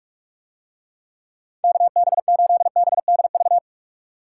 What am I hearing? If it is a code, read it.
Morse code, heard as a keyed tone, about 46 wpm, 700 Hz. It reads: KB8BDV